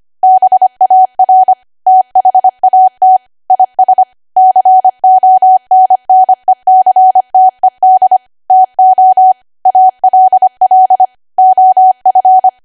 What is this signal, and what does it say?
Morse code, heard as a keyed tone, about 25 wpm, 750 Hz. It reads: BAR THAT IS CONNECTED TO ALL OF
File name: Archive / 25 WPM / 25WPM